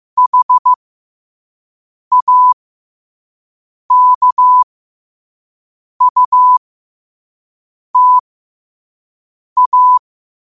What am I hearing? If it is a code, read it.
Morse code, heard as a keyed tone, about 15 wpm, spaced as for 5 wpm, 1000 Hz. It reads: HAKUTA
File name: qotdfiles / 5wpm